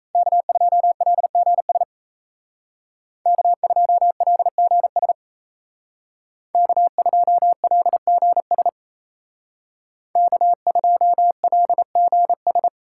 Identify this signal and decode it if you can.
Morse code, heard as a keyed tone, about 35 wpm, 700 Hz. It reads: K2LGS K2LGS K2LGS K2LGS
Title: Speed Racing - USA 1x3 Call Signs 28wpm